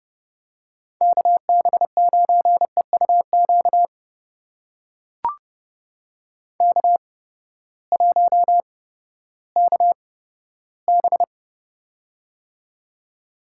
Morse code, heard as a keyed tone, about 30 wpm, spaced as for 15 wpm, 700 Hz. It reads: KB9EUQ K1KB
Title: USA Call Signs - 8x Character Spacing - 30wpm